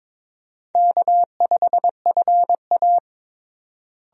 Morse code, heard as a keyed tone, about 22 wpm, 700 Hz. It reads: K5FA
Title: USA Call Signs - 4 Characters - Repeated 2x - 22wpm